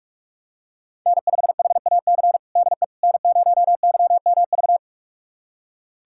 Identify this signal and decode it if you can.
Morse code, heard as a keyed tone, about 46 wpm, 700 Hz. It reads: N5HAC DE N0YMV
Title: USA Callsign DE Callsign - 5 Characters - 45wpm